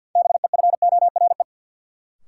Morse code, heard as a keyed tone, about 51 wpm, 700 Hz. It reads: BEFORE
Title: Sets of 2 Words - New Word Drill - 50wpm